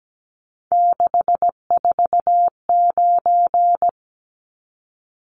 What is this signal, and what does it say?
Morse code, heard as a keyed tone, about 17 wpm, 700 Hz. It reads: 649
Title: Numbers - 3-Digit 17wpm